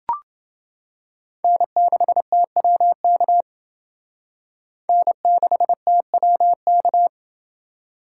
Morse code, heard as a keyed tone, about 29 wpm, 700 Hz. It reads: N6TWK N6TWK